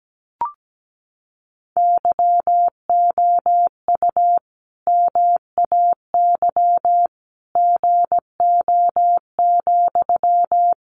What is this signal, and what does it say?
Morse code, heard as a keyed tone, about 17 wpm, 700 Hz. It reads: YOU MAY GO,
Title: Sentences from Top 400 Words 17wpm